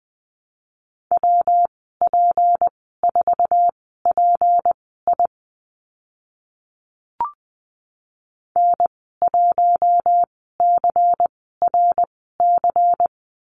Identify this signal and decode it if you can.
Morse code, heard as a keyed tone, about 20 wpm, spaced as for 14 wpm, 700 Hz. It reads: WP4PI N1CRC
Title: 154 Lesson 25g - Letter p - Call Signs 20wpm